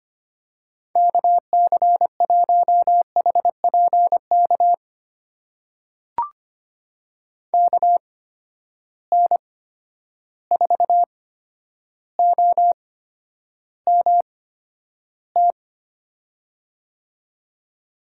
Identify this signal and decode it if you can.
Morse code, heard as a keyed tone, about 25 wpm, spaced as for 11 wpm, 700 Hz. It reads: KC1HPK KN4OMT
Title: USA Call Signs - 6 Characters - 8x Character Spacing - 25wpm